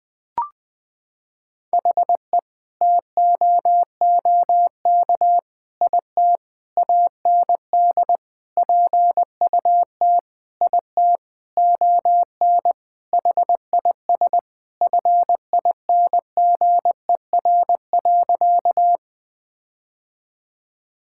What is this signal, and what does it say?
Morse code, heard as a keyed tone, about 21 wpm, 700 Hz. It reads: HE TOOK IT AND PUT IT ON HIS FINGER.
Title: Sentences from Top 1600 Words 20wpm